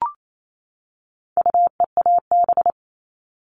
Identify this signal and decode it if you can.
Morse code, heard as a keyed tone, about 28 wpm, 700 Hz. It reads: UEAB
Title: All Cut Numbers - 4-Digit 28wpm